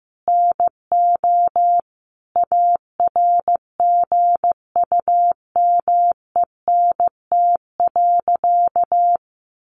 Morse code, heard as a keyed tone, about 15 wpm, 700 Hz. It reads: NO ARGUMENT.
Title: A Sound Of Thunder 15wpm